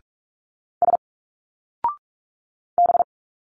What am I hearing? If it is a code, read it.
Morse code, heard as a keyed tone, about 45 wpm, 700 Hz. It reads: S B
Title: Single Letters 45wpm